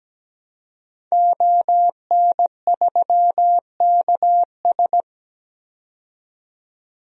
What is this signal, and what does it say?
Morse code, heard as a keyed tone, about 17 wpm, 700 Hz. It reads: ON3KS
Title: DXCC-100 Call Signs - 5 Characters - 17wpm